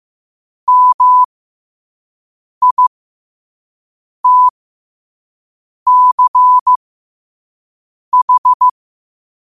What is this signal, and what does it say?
Morse code, heard as a keyed tone, about 15 wpm, spaced as for 5 wpm, 1000 Hz. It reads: MITCH
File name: 5wpm